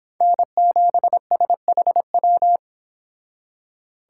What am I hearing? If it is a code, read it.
Morse code, heard as a keyed tone, about 26 wpm, 700 Hz. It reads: N7SHW